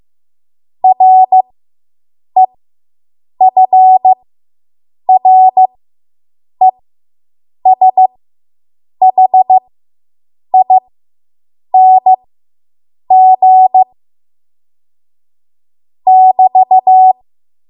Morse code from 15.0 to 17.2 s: <BT>